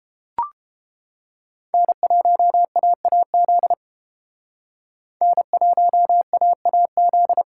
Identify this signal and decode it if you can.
Morse code, heard as a keyed tone, about 31 wpm, 700 Hz. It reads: N1AAZ N1AAZ